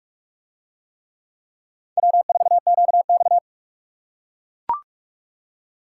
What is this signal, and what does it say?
Morse code, heard as a keyed tone, about 40 wpm, 700 Hz. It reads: W4QX